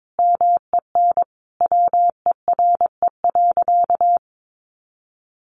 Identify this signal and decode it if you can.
Morse code, heard as a keyed tone, about 22 wpm, 700 Hz. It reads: MEN WERE.
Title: Sentences from Top 1000 Words 22wpm